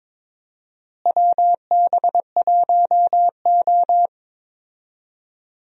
Morse code, heard as a keyed tone, about 21 wpm, 700 Hz. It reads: WB1O
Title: USA Call Signs - 4 Characters - Repeated 3x - 22wpm